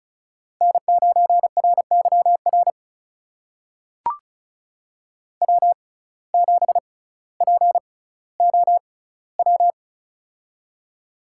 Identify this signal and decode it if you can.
Morse code, heard as a keyed tone, about 36 wpm, spaced as for 18 wpm, 700 Hz. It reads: N9RYR W7POW
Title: USA Call Signs - 5 Characters - 6x Character Spacing - 35wpm